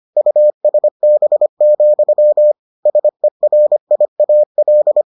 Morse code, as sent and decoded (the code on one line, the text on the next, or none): ..- ... -... --..-- / ... .-. .. .- .-..
USB, SERIAL